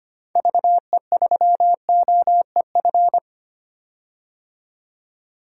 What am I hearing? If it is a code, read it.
Morse code, heard as a keyed tone, about 25 wpm, 690 Hz. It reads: VE3OEF
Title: DXCC-100 Call Signs - 6 Characters - 25wpm